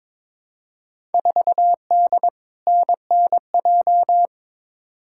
Morse code, heard as a keyed tone, about 22 wpm, 700 Hz. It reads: 4D NNJ
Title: Field Day Exchanges Encore - 22wpm